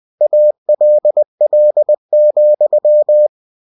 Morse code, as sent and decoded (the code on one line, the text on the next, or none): .- .-.. .-.. --..--
ALL,